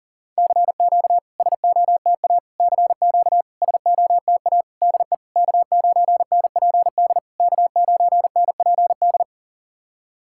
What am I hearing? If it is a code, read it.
Morse code, heard as a keyed tone, about 40 wpm, 690 Hz. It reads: CQ SOTA CQ SOTA DE K9NPD K9NPD
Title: SOTA QSO Practice 40wpm